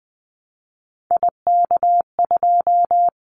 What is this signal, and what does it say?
Morse code, heard as a keyed tone, about 20 wpm, 700 Hz. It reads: IK2
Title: Callsign Prefixes for Top 25 DXCC - Repeated 3x - 20wpm